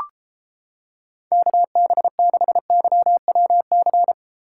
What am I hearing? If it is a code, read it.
Morse code, heard as a keyed tone, about 30 wpm, 700 Hz. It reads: KB6YWC